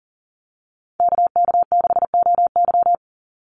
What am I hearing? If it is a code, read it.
Morse code, heard as a keyed tone, about 40 wpm, 700 Hz. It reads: KK6OY
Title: USA Call Signs 40wpm